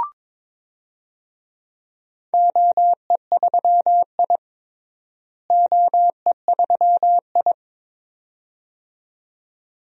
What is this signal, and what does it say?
Morse code, heard as a keyed tone, about 22 wpm, 700 Hz. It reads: OE3I OE3I
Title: DXCC-100 Call Signs - 4 Characters - Repeated 2x - 22wpm